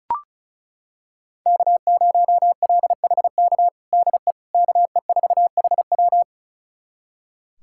Morse code, heard as a keyed tone, about 34 wpm, 700 Hz. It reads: K0LHK DE KE4HW
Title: USA Callsign DE Callsign - 5 Characters - 35wpm